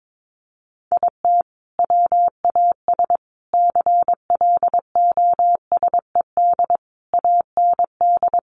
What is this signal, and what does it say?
Morse code, heard as a keyed tone, about 22 wpm, 700 Hz. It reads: IT WAS CLOSED AND
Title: Sentences from Top 1400 Words 22wpm